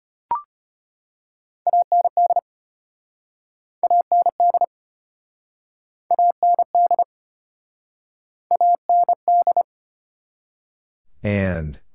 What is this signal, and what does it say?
Morse code, heard as a keyed tone, about 35 wpm, 700 Hz. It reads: AND AND AND AND